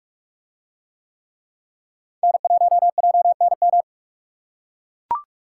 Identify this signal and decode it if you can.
Morse code, heard as a keyed tone, about 43 wpm, 700 Hz. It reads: N1JNM